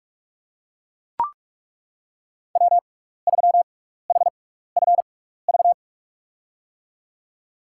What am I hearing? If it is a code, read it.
Morse code, heard as a keyed tone, about 41 wpm, spaced as for 15 wpm, 700 Hz. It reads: W3HFV
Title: USA Call Signs - 5 Characters - 6x Character Spacing - 45wpm